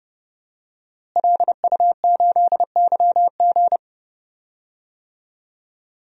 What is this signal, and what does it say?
Morse code, heard as a keyed tone, about 32 wpm, 700 Hz. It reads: LU8YG
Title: DXCC-100 Call Signs - 5 Characters - Repeated 3x - 30wpm